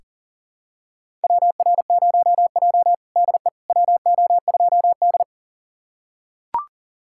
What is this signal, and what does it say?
Morse code, heard as a keyed tone, about 40 wpm, 700 Hz. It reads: WR0J DE WO2D
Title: USA Callsign DE Callsign - 4 Characters - 40wpm